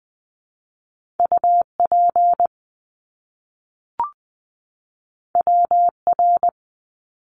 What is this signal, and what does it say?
Morse code, heard as a keyed tone, about 20 wpm, 700 Hz. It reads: UP WR
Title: Top Two Letter Combinations in Words 20wpm